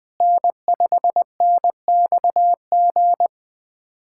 Morse code, heard as a keyed tone, about 20 wpm, 700 Hz. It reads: N5NXG
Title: USA Callsign DE Callsign - 5 Characters - 20wpm